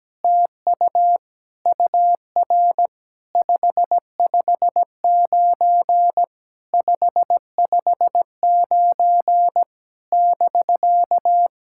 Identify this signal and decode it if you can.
Morse code, heard as a keyed tone, about 17 wpm, 700 Hz. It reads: TU UR 559 559 <BK>
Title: SOTA QSO Practice 17wpm